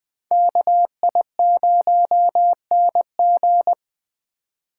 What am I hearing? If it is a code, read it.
Morse code, heard as a keyed tone, about 24 wpm, 700 Hz. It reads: KI0NG
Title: Speed Racing - USA 2x2 Call Signs 20wpm